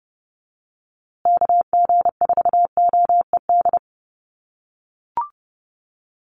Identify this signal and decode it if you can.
Morse code, heard as a keyed tone, about 30 wpm, 700 Hz. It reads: KG4OED
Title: USA Call Signs Repeated 3x 30wpm